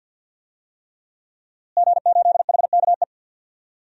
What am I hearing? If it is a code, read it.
Morse code, heard as a keyed tone, about 53 wpm, 700 Hz. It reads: G8HKE
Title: DXCC-100 Call Signs - 5 Characters - Repeated 3x - 50wpm